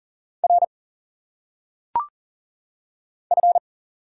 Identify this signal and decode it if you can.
Morse code, heard as a keyed tone, about 38 wpm, 700 Hz. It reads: R F
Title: Single Letters - Phonetic Pronunciation 40wpm